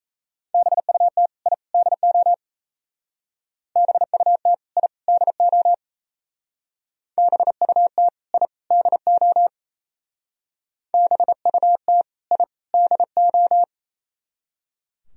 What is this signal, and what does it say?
Morse code, heard as a keyed tone, about 36 wpm, 700 Hz. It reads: BUT I DO BUT I DO BUT I DO BUT I DO